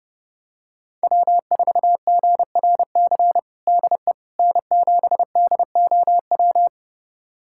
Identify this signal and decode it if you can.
Morse code, heard as a keyed tone, about 32 wpm, 700 Hz. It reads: W4GRC DE N7DOW
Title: USA Callsign DE Callsign - 5 Characters - 30wpm